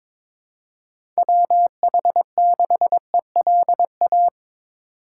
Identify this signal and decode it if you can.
Morse code, heard as a keyed tone, about 22 wpm, 700 Hz. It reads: WH6ELA
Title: USA Call Signs - 6 Characters - 2x Character Spacing - 22wpm